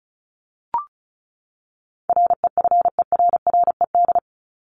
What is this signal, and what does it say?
Morse code, heard as a keyed tone, about 34 wpm, 700 Hz. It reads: REFERRED
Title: ED Suffix - Top 100 Words with Base Words 35wpm